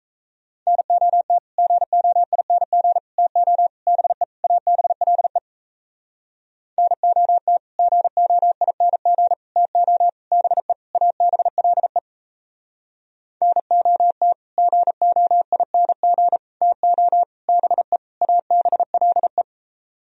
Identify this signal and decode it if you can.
Morse code, heard as a keyed tone, about 37 wpm, 700 Hz. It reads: NOT GOING TO BE ABLE NOT GOING TO BE ABLE NOT GOING TO BE ABLE